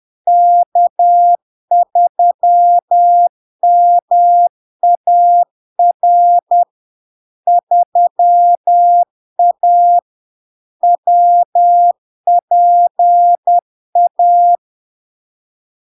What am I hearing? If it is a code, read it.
Morse code, heard as a keyed tone, about 10 wpm, 700 Hz. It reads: K3MAR 3A WPA
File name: LARK-FD-2022-Log-in-CW-10wpm-700-Hz